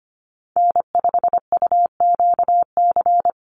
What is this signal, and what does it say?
Morse code, heard as a keyed tone, about 25 wpm, 700 Hz. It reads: N5UQC
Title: USA Call Signs 25wpm